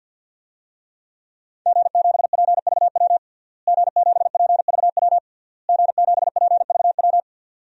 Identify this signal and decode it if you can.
Morse code, heard as a keyed tone, about 47 wpm, 700 Hz. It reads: G7PVW G7PVW G7PVW
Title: DXCC-100 Call Signs - 5 Characters - Repeated 3x - 50wpm